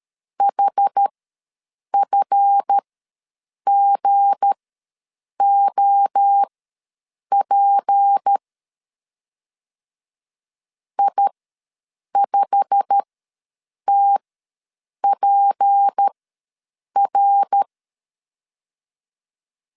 Morse code, heard as a keyed tone, about 13 wpm, spaced as for 7 wpm, 800 Hz. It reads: HFGOP I5TPR
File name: LEZIONI / 8 YJ45 / 8 YJ45 ws 12wpm 3146